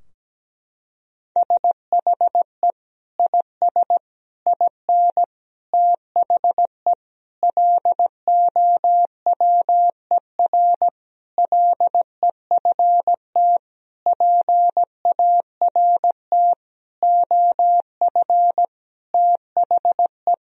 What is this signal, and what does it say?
Morse code, heard as a keyed tone, about 17 wpm, 700 Hz. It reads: SHE IS IN THE LOWER LEFT PART OF THE
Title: Sentences from Top 1000 Words - Review Part 4 - 17wpm